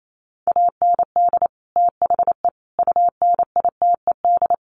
Morse code, heard as a keyed tone, about 28 wpm, 700 Hz. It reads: AND THE UNITED
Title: Sets of 4 Words 28wpm